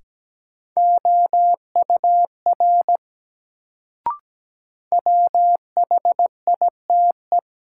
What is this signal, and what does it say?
Morse code, heard as a keyed tone, about 17 wpm, 700 Hz. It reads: OUR WHITE
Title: Sets of 2 Words - New Word Drill - 17wpm